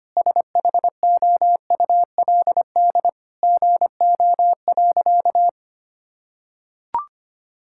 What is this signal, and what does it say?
Morse code, heard as a keyed tone, about 25 wpm, 700 Hz. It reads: SHOULD GO.
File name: practice / 25wpm / Sentences with 4-Word Phrases - 25wpm